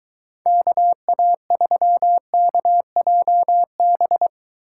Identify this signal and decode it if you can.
Morse code, heard as a keyed tone, about 22 wpm, 700 Hz. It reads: KA3KJB